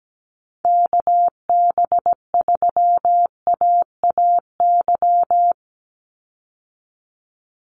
Morse code, heard as a keyed tone, about 17 wpm, 700 Hz. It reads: KB3AAY
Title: Call Signs Repeated 2x - 17wpm